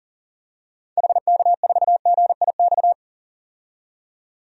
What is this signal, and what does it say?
Morse code, heard as a keyed tone, about 37 wpm, 700 Hz. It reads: HK4GIX
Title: DXCC-100 Call Signs - 6 Characters - 40wpm